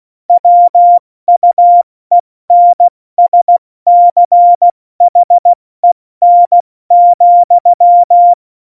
Morse code, heard as a keyed tone, about 16 wpm, 700 Hz. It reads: WUENSCHEN,
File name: Morsekurs / Witze / Witz 17. November 2020, 70 F80